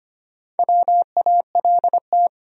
Morse code, heard as a keyed tone, about 25 wpm, 700 Hz. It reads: WALT
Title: USA Common Names - 25wpm